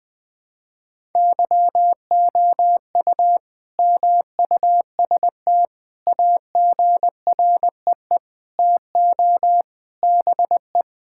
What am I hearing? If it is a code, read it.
Morse code, heard as a keyed tone, about 21 wpm, 700 Hz. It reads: YOU MUST AGREE TO BE
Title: Sentences with 9-Word Phrases - 20wpm